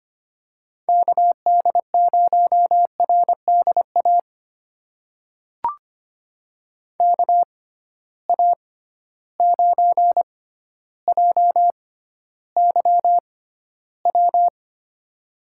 Morse code, heard as a keyed tone, about 25 wpm, spaced as for 13 wpm, 700 Hz. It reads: KD0RDA KA9JYW
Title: USA Call Signs - 6 Characters - 6x Character Spacing - 25wpm